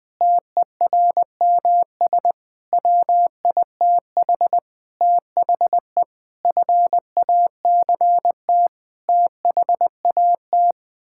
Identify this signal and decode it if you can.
Morse code, heard as a keyed tone, about 20 wpm, 700 Hz. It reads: TERMS WITH THE FACT THAT
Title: Sets of 8 Words - 20wpm